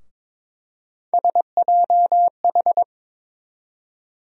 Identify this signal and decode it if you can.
Morse code, heard as a keyed tone, about 21 wpm, 700 Hz. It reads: SJH
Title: USA Call Sign Postfixes - 3 Letters 22wpm